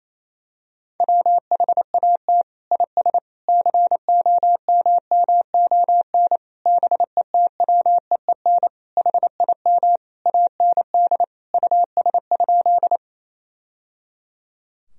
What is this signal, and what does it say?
Morse code, heard as a keyed tone, about 28 wpm, 700 Hz. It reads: WHAT IS COMMON BETWEEN HIM AND US?
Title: Sentences from Top 1000 Words - Review Part 3 - 28wpm